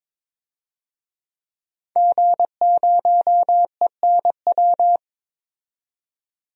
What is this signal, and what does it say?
Morse code, heard as a keyed tone, about 22 wpm, 700 Hz. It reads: G0ENW
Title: DXCC-100 Call Signs - 5 Characters - 22wpm